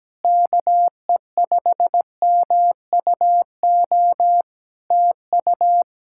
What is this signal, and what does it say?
Morse code, heard as a keyed tone, about 17 wpm, 700 Hz. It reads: KE5MUO TU
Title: K1USN SST QSO Practice 17wpm